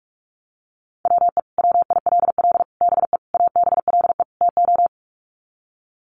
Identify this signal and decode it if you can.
Morse code, heard as a keyed tone, about 44 wpm, 700 Hz. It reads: WE WILL BE ABLE TO
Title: Sets of 5 Words Encore 45wpm